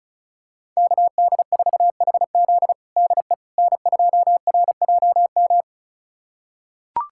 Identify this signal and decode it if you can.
Morse code, heard as a keyed tone, about 36 wpm, 700 Hz. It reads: KD4HZ DE N2RJM
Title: USA Callsign DE Callsign - 5 Characters - 35wpm